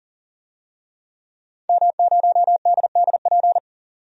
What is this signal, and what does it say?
Morse code, heard as a keyed tone, about 40 wpm, 700 Hz. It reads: M0DDP
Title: DXCC-100 Call Signs - 5 Characters - 40wpm